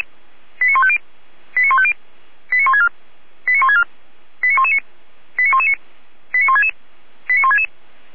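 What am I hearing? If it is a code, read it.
Five-tone selective call (ZVEI), heard as quick runs of five tones, tones of about 70 ms. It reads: 89150 89150 89164 89164 89119 89119 89166 89166
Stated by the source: ZVEI five-tone selective call received on an SDR.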